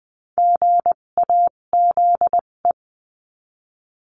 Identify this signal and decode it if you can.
Morse code, heard as a keyed tone, about 20 wpm, spaced as for 17 wpm, 700 Hz. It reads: GAZE
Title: 257 Lesson 42b - Words 14-20wpm